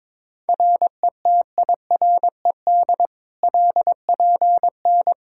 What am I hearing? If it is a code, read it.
Morse code, heard as a keyed tone, about 22 wpm, 700 Hz. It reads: RETIRED LPN
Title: Practice QSO - Exchange 3 - More Info - 22wpm